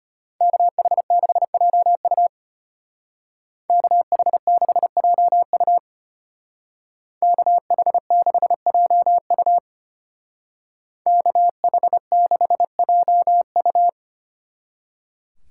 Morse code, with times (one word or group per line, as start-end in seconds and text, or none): 0.4-2.3 KH6JU
3.7-5.8 KH6JU
7.2-9.6 KH6JU
11.1-13.9 KH6JU